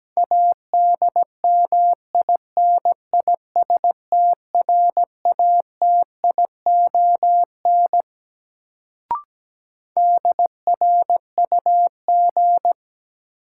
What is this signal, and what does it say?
Morse code, heard as a keyed tone, about 17 wpm, 700 Hz. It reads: ADMINISTRATION DRUG
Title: Sets of 5 Words - New Word Drill - 17wpm